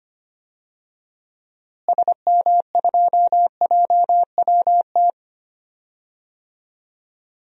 Morse code, heard as a keyed tone, about 25 wpm, 700 Hz. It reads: SM2JWT
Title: DXCC-100 Call Signs - 6 Characters - Repeated 2x - 25wpm